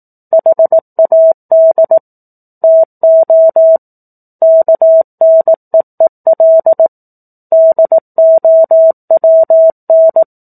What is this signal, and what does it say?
Morse code, heard as a keyed tone, about 18 wpm, 650 Hz. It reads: HAD TO KNEEL DOWN
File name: 04-Chapter-04 18 WPM